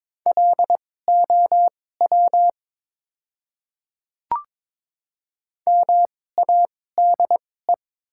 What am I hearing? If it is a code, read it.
Morse code, heard as a keyed tone, about 22 wpm, spaced as for 16 wpm, 700 Hz. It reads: LOW MADE
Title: 2x Character Spacing - Top 200 Words - 22wpm